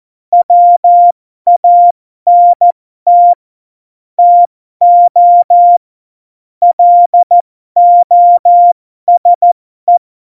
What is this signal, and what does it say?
Morse code, heard as a keyed tone, about 14 wpm, spaced as for 12 wpm, 700 Hz. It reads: WANT TO LOSE
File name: Morsekurs / Witze